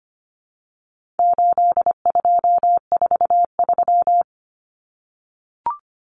Morse code, heard as a keyed tone, about 25 wpm, 700 Hz. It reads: 8243
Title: Common Cut Numbers - 4-Digit 25wpm